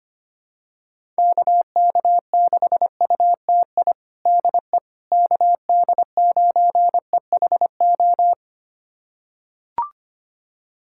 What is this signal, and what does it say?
Morse code, heard as a keyed tone, about 25 wpm, 700 Hz. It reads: KK6UTI DE KD9EHO
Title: USA Callsign DE Callsign 25wpm